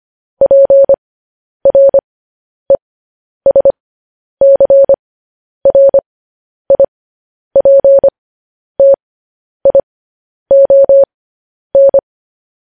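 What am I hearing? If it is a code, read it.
Morse code, heard as a keyed tone, about 25 wpm, spaced as for 10 wpm, 550 Hz. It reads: PRESCRIPTION
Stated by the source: Morse code news headlines practice at 10 wpm.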